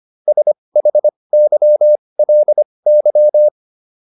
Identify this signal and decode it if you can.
Morse code, heard as a keyed tone, about 25 wpm, spaced as for 20 wpm, 600 Hz. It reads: SHYLY